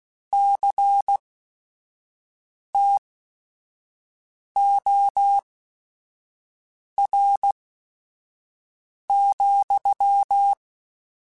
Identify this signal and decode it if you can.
Morse code, heard as a keyed tone, about 16 wpm, spaced as for 5 wpm, 800 Hz. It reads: CTOR,
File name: Technical Subjects / Technical Subjects-mp3s / Reactance @5